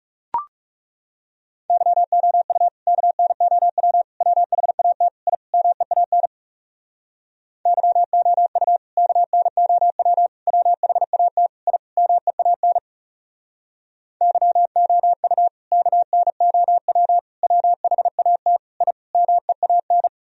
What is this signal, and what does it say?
Morse code, heard as a keyed tone, about 40 wpm, 700 Hz. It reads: YOU KNOW WHAT I MEAN YOU KNOW WHAT I MEAN YOU KNOW WHAT I MEAN